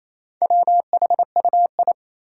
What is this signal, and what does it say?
Morse code, heard as a keyed tone, about 27 wpm, 700 Hz. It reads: WHUI
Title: Four Challenging Letter-Number Combinations 28wpm